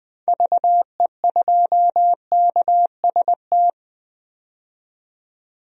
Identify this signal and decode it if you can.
Morse code, heard as a keyed tone, about 20 wpm, 700 Hz. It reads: VE2KST
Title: DXCC-100 Call Signs - 6 Characters - 20wpm